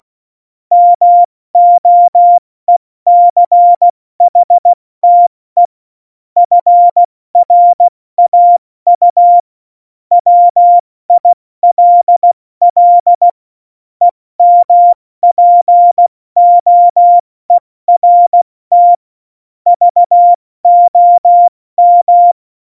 Morse code, heard as a keyed tone, about 16 wpm, 700 Hz. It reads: MOECHTE FRAU WILL EMPOERT VOM
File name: Morsekurs / Witze / Witz 09. Juli 2019, 70 F80